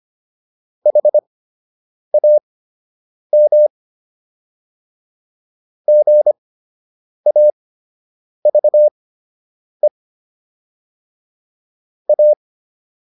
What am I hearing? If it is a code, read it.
Morse code, heard as a keyed tone, about 25 wpm, spaced as for 8 wpm, 600 Hz. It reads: HAM GAVE A